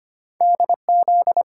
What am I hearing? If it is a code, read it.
Morse code, heard as a keyed tone, about 25 wpm, 700 Hz. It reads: DZ
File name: Two Letter-Number Combinations - 25wpm